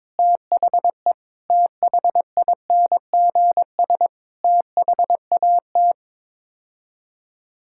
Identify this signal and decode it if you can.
Morse code, heard as a keyed tone, about 22 wpm, 700 Hz. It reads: THE THINGS THAT
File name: practice / 22wpm / Build A Phrase - 5-Word Phrases - 22wpm